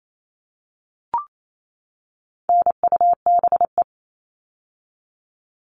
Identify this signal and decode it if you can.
Morse code, heard as a keyed tone, about 28 wpm, 700 Hz. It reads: NUBE
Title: All Cut Numbers - 4-Digit 28wpm